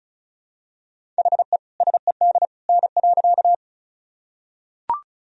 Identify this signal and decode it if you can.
Morse code, heard as a keyed tone, about 36 wpm, 700 Hz. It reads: HE SED N.
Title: Sentences from Top 100 Words and Abbreviations 35wpm